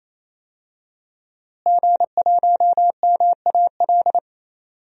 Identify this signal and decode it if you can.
Morse code, heard as a keyed tone, about 28 wpm, 700 Hz. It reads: G1MAL
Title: DXCC-100 Call Signs - 5 Characters - Repeated 3x - 28wpm